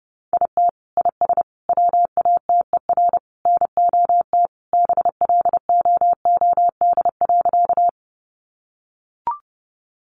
Morse code, heard as a keyed tone, about 30 wpm, 700 Hz. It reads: IT IS WATER NOT BLOOD.
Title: Sentences from Top 1100 Words 30wpm